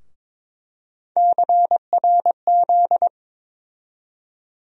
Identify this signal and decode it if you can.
Morse code, heard as a keyed tone, about 22 wpm, 700 Hz. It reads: CRZ
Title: USA Call Sign Postfixes - 3 Letters 22wpm